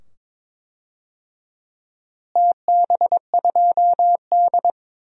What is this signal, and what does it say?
Morse code, heard as a keyed tone, about 22 wpm, 700 Hz. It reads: TB2D
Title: DXCC-100 Call Signs - 4 Characters - Repeated 2x - 22wpm